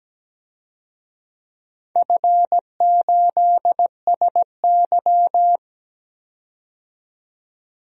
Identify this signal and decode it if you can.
Morse code, heard as a keyed tone, about 17 wpm, 700 Hz. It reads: F8SY
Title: DXCC-100 Call Signs - 4 Characters - 17wpm